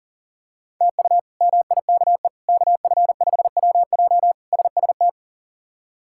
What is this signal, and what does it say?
Morse code, heard as a keyed tone, about 40 wpm, 700 Hz. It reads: TU MIKE KF5WJ SST
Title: K1USN SST QSO Practice - 40wpm